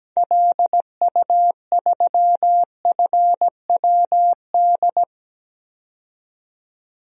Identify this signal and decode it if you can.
Morse code, heard as a keyed tone, about 17 wpm, 700 Hz. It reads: LU3FWD
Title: DXCC-100 Call Signs - 6 Characters - 17wpm